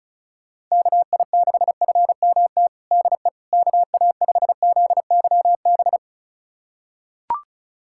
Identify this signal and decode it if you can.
Morse code, heard as a keyed tone, about 34 wpm, 700 Hz. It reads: KI6FMT DE KA5ZYB
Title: USA Callsign DE Callsign - 6 Characters - 35wpm